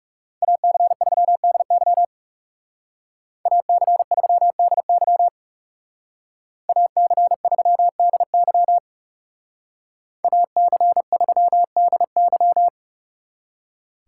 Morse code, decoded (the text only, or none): AC3DY AC3DY AC3DY AC3DY